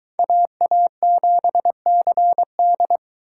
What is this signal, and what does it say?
Morse code, heard as a keyed tone, about 23 wpm, 700 Hz. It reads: AA7CD